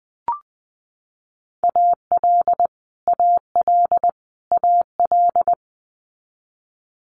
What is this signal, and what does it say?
Morse code, heard as a keyed tone, about 20 wpm, 700 Hz. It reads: AL AL AL
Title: Multi-letter Phonemes Repeated 3x 20wpm